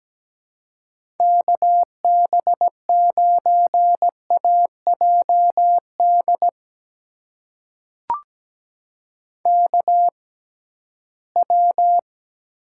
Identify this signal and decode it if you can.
Morse code, heard as a keyed tone, about 17 wpm, spaced as for 13 wpm, 700 Hz. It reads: KB9AJD KW
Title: USA Call Signs - 6x Character Spacing - 17wpm